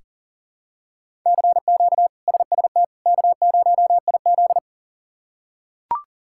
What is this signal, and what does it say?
Morse code, heard as a keyed tone, about 40 wpm, 700 Hz. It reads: CQ SST K0IZ
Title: K1USN SST QSO Practice - 40wpm